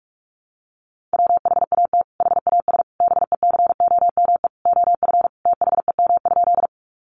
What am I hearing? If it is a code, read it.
Morse code, heard as a keyed tone, about 44 wpm, 700 Hz. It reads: WHAT HAS BECOME OF THEM?
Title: Sentences from Top 600 Words 45wpm